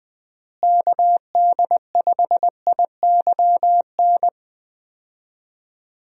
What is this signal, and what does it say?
Morse code, heard as a keyed tone, about 20 wpm, 700 Hz. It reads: KD5IYN
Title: USA Call Signs - 6 Characters - Repeated 3x - 20wpm